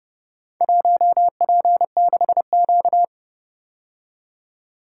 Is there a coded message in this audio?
Yes: Morse code from 0.6 to 3.0 s, heard as a keyed tone, about 29 wpm, 700 Hz. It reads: 1P6Q